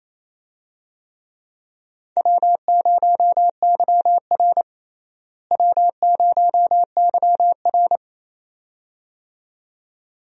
Morse code, heard as a keyed tone, about 28 wpm, 700 Hz. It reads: W0YR W0YR
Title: USA Call Signs - 4 Characters - Repeated 2x - 28wpm